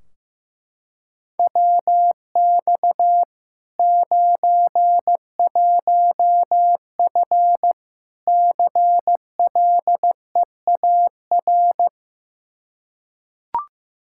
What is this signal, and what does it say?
Morse code, heard as a keyed tone, about 15 wpm, 700 Hz. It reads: WX 91F CLEAR
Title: Practice QSO - Exchange 2 - Info - 15wpm